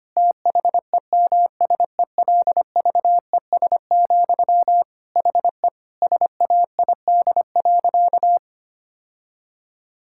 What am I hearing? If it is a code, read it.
Morse code, heard as a keyed tone, about 25 wpm, 700 Hz. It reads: THEMSELVES, HE SAID.
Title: Sentences from Top 1000 Words - Review Part 6 - 25wpm